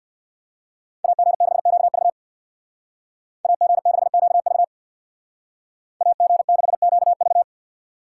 Morse code, heard as a keyed tone, about 58 wpm, 700 Hz. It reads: AG6QV AG6QV AG6QV
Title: Speed Racing - USA 2x2 Call Signs 50wpm